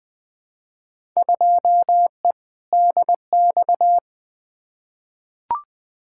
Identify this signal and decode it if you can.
Morse code, heard as a keyed tone, about 21 wpm, 700 Hz. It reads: 2E DX
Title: Field Day Exchanges - 20wpm